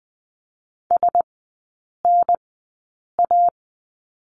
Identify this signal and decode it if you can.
Morse code, heard as a keyed tone, about 20 wpm, spaced as for 8 wpm, 700 Hz. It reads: SNA
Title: 087 Lesson 15e - Number 5 - Three Characters 20wpm